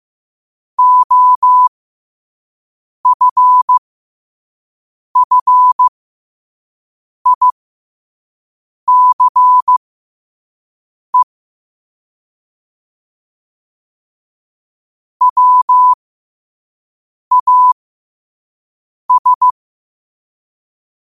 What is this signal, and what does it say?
Morse code, heard as a keyed tone, about 15 wpm, spaced as for 5 wpm, 1000 Hz. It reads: OFFICE WAS
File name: qotdfiles / 5wpm